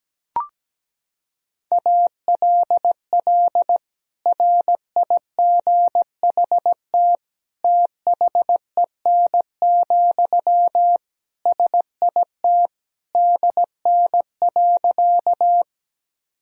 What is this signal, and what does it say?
Morse code, heard as a keyed tone, about 17 wpm, 700 Hz. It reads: ALL RIGHT THEN, SIT DN.
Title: Sentences from Top 500 Words and Abbreviations 17wpm